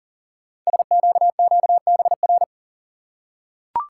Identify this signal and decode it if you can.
Morse code, heard as a keyed tone, about 40 wpm, 700 Hz. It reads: SQQBR
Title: Code Groups - 5 Letters - 40wpm